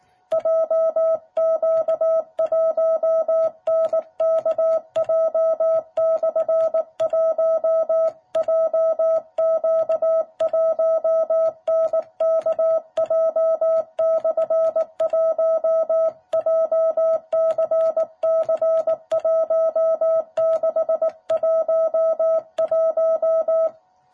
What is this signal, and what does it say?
Morse code, heard as a keyed tone, about 18 wpm, 660 Hz. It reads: JQ1NKJ/1 JQ1NKJ/1 JCC1611
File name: jh8cbh / 2021cw